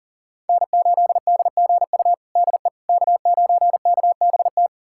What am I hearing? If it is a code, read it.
Morse code, heard as a keyed tone, about 40 wpm, 700 Hz. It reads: N8DGU DE K9KBT
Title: USA Callsign DE Callsign - 5 Characters - 40wpm